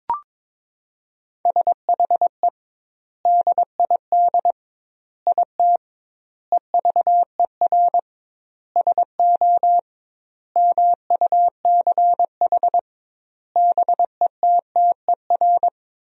Morse code, heard as a keyed tone, about 22 wpm, 700 Hz. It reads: SHE DID IT EVER SO MUCH BETTER
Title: 2x Word-Spacing - Sentences from Top 400 Words - 22wpm